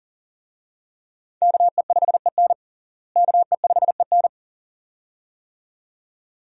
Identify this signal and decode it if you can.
Morse code, heard as a keyed tone, about 40 wpm, 700 Hz. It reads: KE5EN KE5EN
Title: USA Call Signs - 5 Characters - Repeated 2x - 40wpm